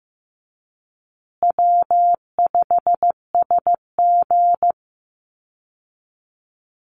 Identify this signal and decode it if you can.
Morse code, heard as a keyed tone, about 15 wpm, 700 Hz. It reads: W5SG
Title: Easy Call Signs Repeated 2x - 15wpm